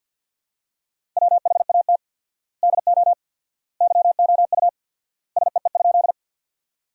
Morse code, heard as a keyed tone, about 50 wpm, 700 Hz. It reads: WHAT DO YOU SEE?
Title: 4x Word-Spacing - Sentences from Top 100 Words - 50wpm